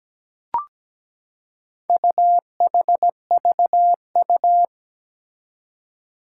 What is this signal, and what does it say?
Morse code, heard as a keyed tone, about 17 wpm, 700 Hz. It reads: UHVU
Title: Four Challenging Letter-Number Combinations 17wpm